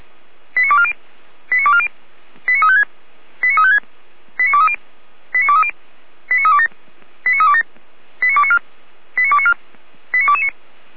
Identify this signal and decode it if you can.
Five-tone selective call (ZVEI), heard as quick runs of five tones, tones of about 70 ms. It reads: 89240 89240 89356 89356 89230 89230 89327 89327 89294 89294 89229